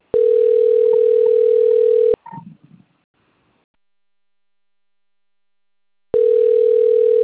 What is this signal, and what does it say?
Ringback tone (North America), 440 + 480 Hz.